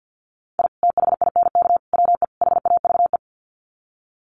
Morse code, heard as a keyed tone, about 50 wpm, 700 Hz. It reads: I THINK WE HAVE
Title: Sets of 4 Words 50wpm